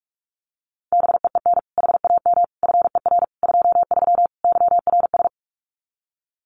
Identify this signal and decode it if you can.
Morse code, heard as a keyed tone, about 46 wpm, 700 Hz. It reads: BEEN HAM FER 23 YRS
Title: Common QSO Phrases Encore 45wpm